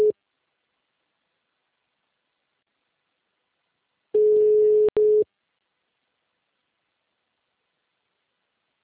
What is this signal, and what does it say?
Ringback tone (Europe, 425 Hz).
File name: out- -2029- - - .22232